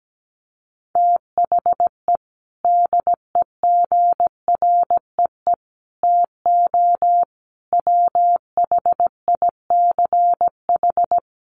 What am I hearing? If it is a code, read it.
Morse code, heard as a keyed tone, about 17 wpm, 700 Hz. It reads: THE DEGREE TO WHICH
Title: Sets of 5 Words Encore2 17wpm